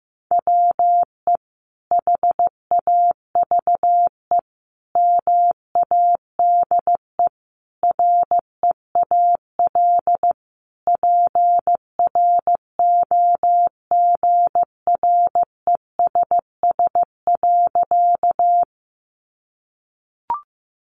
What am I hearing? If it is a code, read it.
Morse code, heard as a keyed tone, about 15 wpm, 700 Hz. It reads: WE HAVE MADE REAL PROGRESS.
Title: Sentences from Top 1400 Words 15wpm